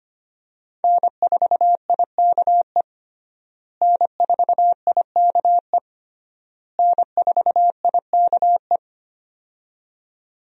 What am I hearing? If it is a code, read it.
Morse code, heard as a keyed tone, about 25 wpm, 700 Hz. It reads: N4IKE N4IKE N4IKE
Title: USA Call Signs - 5 Characters - Repeated 3x - 25wpm